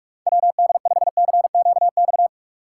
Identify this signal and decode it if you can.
Morse code, heard as a keyed tone, about 57 wpm, 700 Hz. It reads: WD5CQX